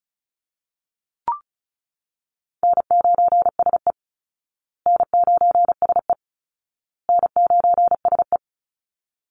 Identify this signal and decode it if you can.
Morse code, heard as a keyed tone, about 34 wpm, 700 Hz. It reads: N9SE N9SE N9SE
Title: Easy Call Signs Repeated 3x - 35wpm